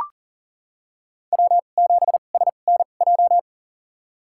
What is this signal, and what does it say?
Morse code, heard as a keyed tone, about 40 wpm, spaced as for 29 wpm, 700 Hz. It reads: W7SNJ
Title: USA Call Signs - 5 Characters - 2x Character Spacing - 40wpm